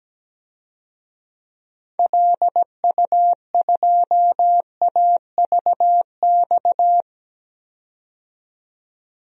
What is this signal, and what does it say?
Morse code, heard as a keyed tone, about 17 wpm, 700 Hz. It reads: LU2AVX
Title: DXCC-100 Call Signs - 6 Characters - 17wpm